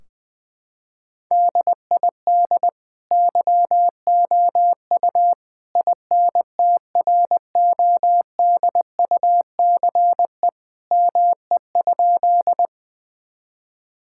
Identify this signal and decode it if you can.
Morse code, heard as a keyed tone, about 20 wpm, 700 Hz. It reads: DID YOU INTRODUCE ME?
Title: Sentences from Top 1600 Words 20wpm